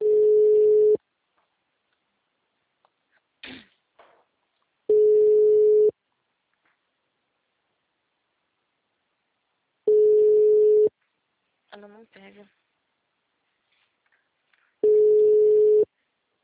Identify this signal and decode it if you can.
Ringback tone (Europe, 425 Hz), with speech.
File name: out- -2001- - - .83